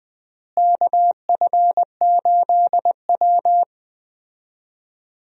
Morse code, heard as a keyed tone, about 20 wpm, 700 Hz. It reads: KF8W
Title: USA Call Signs - 4 Characters - Repeated 3x - 20wpm